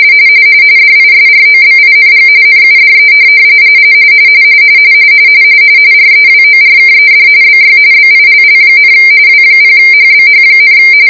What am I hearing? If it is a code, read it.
RTTY, about 45.45 baud. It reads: DISROBING NUDE REWORKED BY BILL, W0HBH,